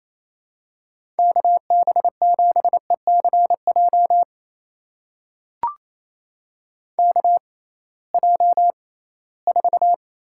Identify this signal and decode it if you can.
Morse code, heard as a keyed tone, about 28 wpm, spaced as for 18 wpm, 700 Hz. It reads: KB7ECJ KJ4
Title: USA Call Signs - 6 Characters - 6x Character Spacing - 28wpm